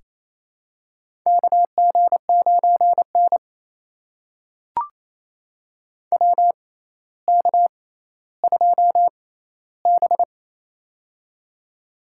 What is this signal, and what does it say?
Morse code, heard as a keyed tone, about 28 wpm, spaced as for 14 wpm, 700 Hz. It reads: KG9N WK2B
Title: USA Call Signs - 4 Characters - 6x Character Spacing - 28wpm